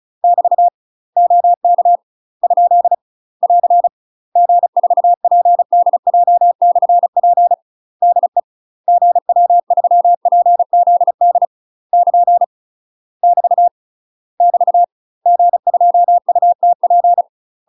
Morse code, heard as a keyed tone, about 36 wpm, 700 Hz. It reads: <BT> OK ? <AR> G4PDJ/P DE GW3PZD <KN> <BT> <BT> G2UTP
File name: UK MASTER QSOs TRAINING / 35WPM / UK MASTER QSOs 0022